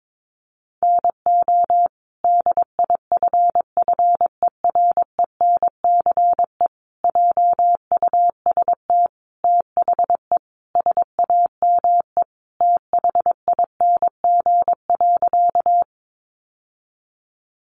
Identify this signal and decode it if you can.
Morse code, heard as a keyed tone, about 22 wpm, 700 Hz. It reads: NO DIFFERENCE JUST THE SAME THING.
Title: Sentences from Top 800 Words 22wpm